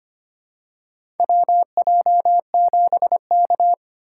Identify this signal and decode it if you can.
Morse code, heard as a keyed tone, about 25 wpm, 700 Hz. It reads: WJ7K
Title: USA Call Signs - 4 Characters - 2x Character Spacing - 25wpm